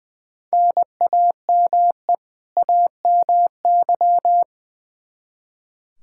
Morse code, heard as a keyed tone, about 20 wpm, 700 Hz. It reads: NAME AMY
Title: Practice QSO - Exchange 1 - Beginning - 20wpm